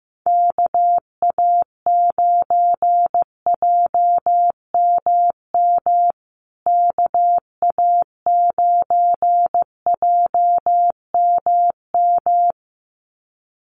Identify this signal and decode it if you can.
Morse code, heard as a keyed tone, about 15 wpm, 700 Hz. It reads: KA9JMM KA9JMM
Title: USA Call Signs Repeated 2x 15wpm